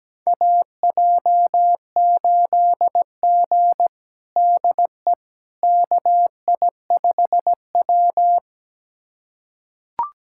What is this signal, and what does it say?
Morse code, heard as a keyed tone, about 17 wpm, 700 Hz. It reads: AJ8G DE KI5W
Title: USA Callsign DE Callsign - 4 Characters - 17wpm